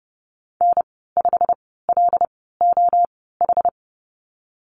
Morse code, heard as a keyed tone, about 32 wpm, spaced as for 17 wpm, 700 Hz. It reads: N5LOH
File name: course / 30:10wpm / 116 Lesson 19g - Letter w - Call Signs 30wpm